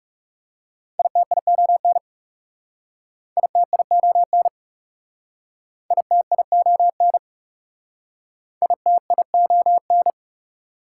Morse code, heard as a keyed tone, about 37 wpm, 700 Hz. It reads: ITION ITION ITION ITION
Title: Speed-Racing Prefixes and Suffixes 30wpm